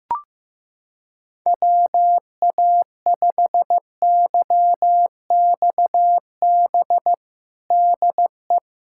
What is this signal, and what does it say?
Morse code, heard as a keyed tone, about 15 wpm, 700 Hz. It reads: WA5YXB DE
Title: USA Callsign DE Callsign - 6 Characters - 15wpm